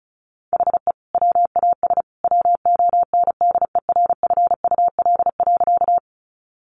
Morse code, heard as a keyed tone, about 34 wpm, 700 Hz. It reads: HE WAS WONDERFUL.
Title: Sentences from Top 1100 Words 35wpm